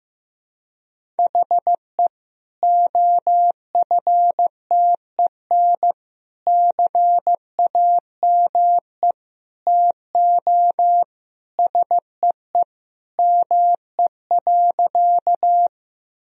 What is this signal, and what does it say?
Morse code, heard as a keyed tone, about 15 wpm, 700 Hz. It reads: HE OFTEN CAME TO SEE ME.
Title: Sentences from Top 1000 Words - Review Part 1 - 15wpm